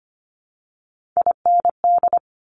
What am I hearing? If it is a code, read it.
Morse code, heard as a keyed tone, about 25 wpm, 700 Hz. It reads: IND
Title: Top Three Letter Combinations in Words 25wpm